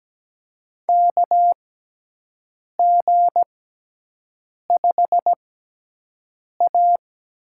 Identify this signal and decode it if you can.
Morse code, heard as a keyed tone, about 17 wpm, spaced as for 6 wpm, 700 Hz. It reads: KG5A
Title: USA Call Signs - 6 Characters - 6x Character Spacing - 17wpm